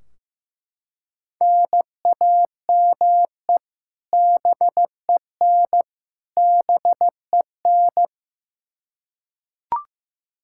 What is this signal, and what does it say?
Morse code, heard as a keyed tone, about 15 wpm, 700 Hz. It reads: NAME BEN BEN
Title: Practice QSO - Exchange 1 - Beginning - 15wpm